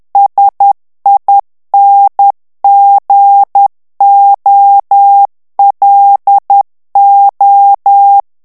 Morse code, heard as a keyed tone, about 10 wpm, 800 Hz. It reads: SINGOLO
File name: ITALIEN texte clair 2 vitesse 10 mots minute